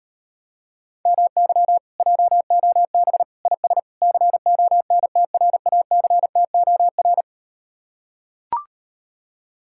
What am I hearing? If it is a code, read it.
Morse code, heard as a keyed tone, about 39 wpm, 700 Hz. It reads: MY JOB IS CONTRACTOR